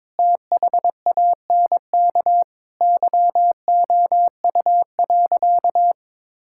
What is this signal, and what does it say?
Morse code, heard as a keyed tone, about 22 wpm, 700 Hz. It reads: THANK YOU.
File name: practice / 22wpm / Sentences with 8-Word Phrases - 22wpm